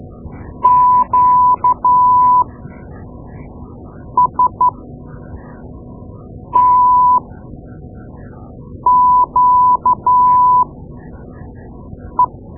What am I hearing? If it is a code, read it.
Morse code, heard as a keyed tone, about 12 wpm, spaced as for 4 wpm, 1000 Hz. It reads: QSTQE
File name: Morse Lessons / Morse Lessons / Tape-2A